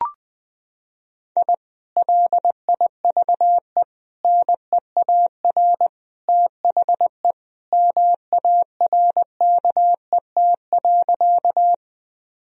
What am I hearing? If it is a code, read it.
Morse code, heard as a keyed tone, about 20 wpm, 700 Hz. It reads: I LIVE NEAR THE MARKET.
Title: Sentences from Top 1000 Words - Review Part 3 - 20wpm